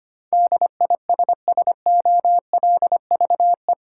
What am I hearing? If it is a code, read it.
Morse code, heard as a keyed tone, about 25 wpm, 700 Hz. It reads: DISSOLVE
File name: Eight Letter Words - 25wpm